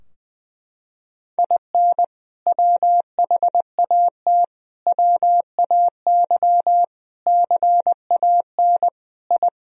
Morse code, heard as a keyed tone, about 20 wpm, 700 Hz. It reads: IN WHAT WAY CAN I
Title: Sentences from Top 1000 Words - Review Part 4 - 20wpm